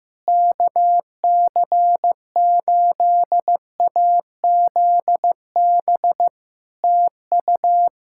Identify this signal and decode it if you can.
Morse code, heard as a keyed tone, about 15 wpm, 700 Hz. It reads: KC8AZB TU
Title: Practice QSO - Exchange 4 - Ending - 15wpm